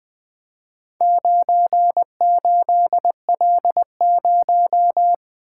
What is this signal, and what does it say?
Morse code, heard as a keyed tone, about 20 wpm, 700 Hz. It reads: 98L0
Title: Four Letter-Number Combinations - 20wpm